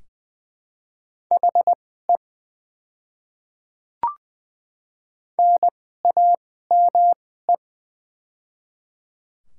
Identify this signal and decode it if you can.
Morse code, heard as a keyed tone, about 20 wpm, spaced as for 14 wpm, 700 Hz. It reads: HE NAME
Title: 2x Character Spacing - Top 200 Words - 20wpm